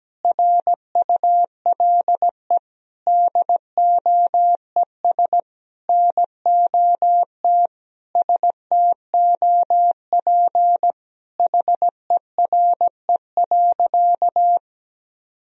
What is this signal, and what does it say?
Morse code, heard as a keyed tone, about 17 wpm, 700 Hz. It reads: RULE DOES NOT STOP HERE.
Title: Sentences from Top 1000 Words - Review Part 2 - 17wpm